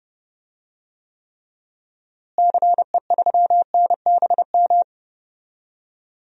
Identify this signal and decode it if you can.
Morse code, heard as a keyed tone, about 32 wpm, 700 Hz. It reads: CE3NBM
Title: DXCC-100 Call Signs - 6 Characters - Repeated 3x - 30wpm